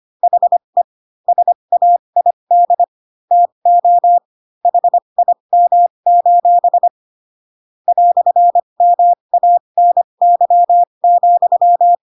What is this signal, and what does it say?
Morse code, heard as a keyed tone, about 25 wpm, spaced as for 22 wpm, 700 Hz. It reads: HE SAID TO HIM: "MANY,